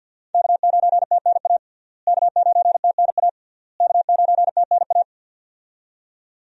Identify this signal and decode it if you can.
Morse code, heard as a keyed tone, about 52 wpm, 700 Hz. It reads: K9TNA K9TNA K9TNA
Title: USA Call Signs - 5 Characters - Repeated 3x - 50wpm